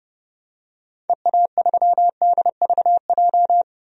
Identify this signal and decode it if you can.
Morse code, heard as a keyed tone, about 32 wpm, 690 Hz. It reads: EA3DVJ
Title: DXCC-100 Call Signs - 6 Characters - 30wpm